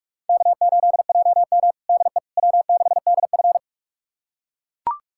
Morse code, heard as a keyed tone, about 41 wpm, 700 Hz. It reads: K8JM DE W6DF
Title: USA Callsign DE Callsign - 4 Characters - 45wpm